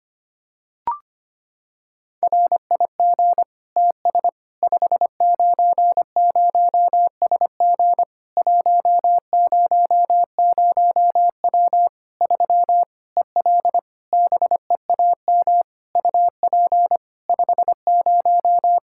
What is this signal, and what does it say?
Morse code, heard as a keyed tone, about 25 wpm, 700 Hz. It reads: RIG TS 590SG 100W 3 EL BEAM UP 50
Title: K1USN SST QSO Practice 25wpm